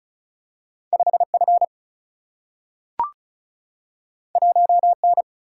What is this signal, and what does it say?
Morse code, heard as a keyed tone, about 36 wpm, 700 Hz. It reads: 5 F 1 N
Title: Single-Letter Single-Number 35wpm